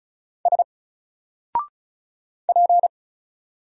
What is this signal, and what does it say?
Morse code, heard as a keyed tone, about 34 wpm, 700 Hz. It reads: S P
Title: Single Letter-Number Alt 35wpm